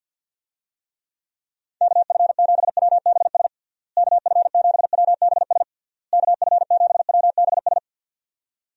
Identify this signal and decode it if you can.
Morse code, heard as a keyed tone, about 48 wpm, 700 Hz. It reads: KF7WBS KF7WBS KF7WBS
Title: USA Call Signs - 6 Characters - Repeated 3x - 50wpm